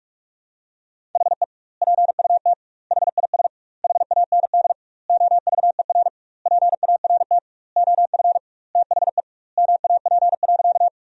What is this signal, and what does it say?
Morse code, heard as a keyed tone, about 46 wpm, 700 Hz. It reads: HE PUT HIS HAND OVER PART OF THE MAP.
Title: 2x Word-Spacing - Sentences from Top 400 Words - 45wpm